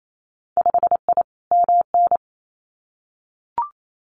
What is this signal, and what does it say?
Morse code, heard as a keyed tone, about 28 wpm, 700 Hz. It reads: 5I MN
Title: Winter Field Day Exchanges Encore 28wpm